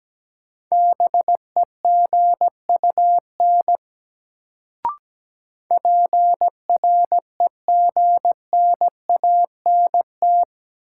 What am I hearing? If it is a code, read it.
Morse code, heard as a keyed tone, about 17 wpm, 700 Hz. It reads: BEGUN PREGNANT
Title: Sentences from Top 1000 Words - Review Part 6 - New Word Drill - 17wpm